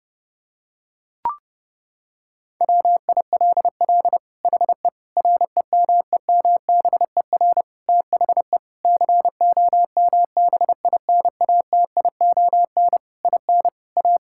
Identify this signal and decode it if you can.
Morse code, heard as a keyed tone, about 30 wpm, 700 Hz. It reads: WILL HE REMEMBER THE COMBINATION IN A
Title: Sentences from Top 1000 Words - Review Part 6 - 30wpm